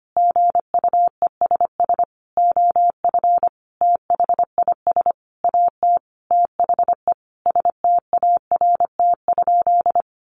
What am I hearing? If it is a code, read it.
Morse code, heard as a keyed tone, about 25 wpm, 700 Hz. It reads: GUESS OF THIS AT THE START?
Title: Sentences from Top 1500 Words 25wpm